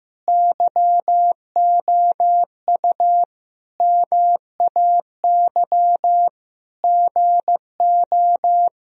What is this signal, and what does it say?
Morse code, heard as a keyed tone, about 16 wpm, 700 Hz. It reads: YOU MAY GO
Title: Sentences from Top 1000 Words - Review Part 4 - 15wpm